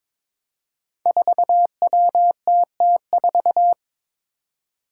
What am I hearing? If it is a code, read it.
Morse code, heard as a keyed tone, about 23 wpm, 700 Hz. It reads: 4WTT4
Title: Five Challenging Letter-Number Combinations 22wpm